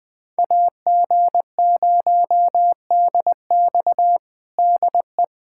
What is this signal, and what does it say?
Morse code, heard as a keyed tone, about 20 wpm, 700 Hz. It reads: AG0DX DE
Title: USA Callsign DE Callsign - 5 Characters - 20wpm